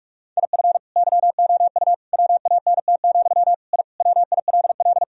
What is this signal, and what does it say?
Morse code, heard as a keyed tone, about 46 wpm, 700 Hz. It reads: IF YOU WANT, I WILL
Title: Sentences from Top 1000 Words - Review Part 3 - 45wpm